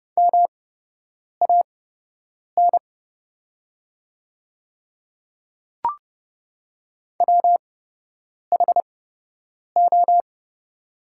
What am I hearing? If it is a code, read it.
Morse code, heard as a keyed tone, about 30 wpm, spaced as for 8 wpm, 700 Hz. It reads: MAN WHO
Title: 8x Character Spacing - Top 200 Words - 30wpm